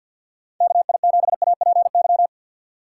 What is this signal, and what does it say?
Morse code, heard as a keyed tone, about 48 wpm, 700 Hz. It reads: KI7APY
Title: USA Call Signs - 2x Character Spacing - 50wpm